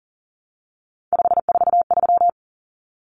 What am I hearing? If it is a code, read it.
Morse code, heard as a keyed tone, about 40 wpm, 700 Hz. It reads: 543
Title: Numbers - 3-Digit 40wpm